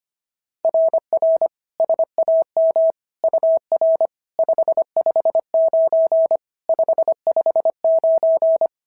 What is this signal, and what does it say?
Morse code, heard as a keyed tone, about 25 wpm, 690 Hz. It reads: RR SAM UR 559 559
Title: SOTA QSO Practice 25wpm